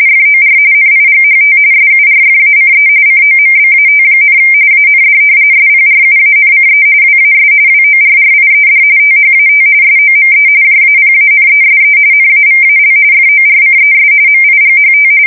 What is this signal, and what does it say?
RTTY, about 45.45 baud. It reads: FOR NIGHT SURFACE COMBAT, AND SPRUANCE WAS WORRIED THAT SOME JAPANESE NAVAL FORCES MIGHT
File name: fsk170 / battleshipnj